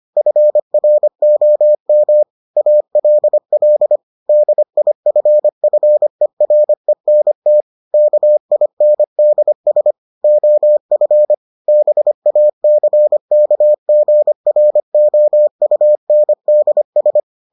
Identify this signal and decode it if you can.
Morse code, heard as a keyed tone, about 25 wpm, 600 Hz. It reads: FROM ALL DIFFERENT KINDS OF BACKGROUNDS